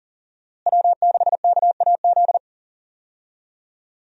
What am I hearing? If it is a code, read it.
Morse code, heard as a keyed tone, about 40 wpm, 700 Hz. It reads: W6KAZ
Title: USA Call Signs - 5 Characters - Repeated 3x - 40wpm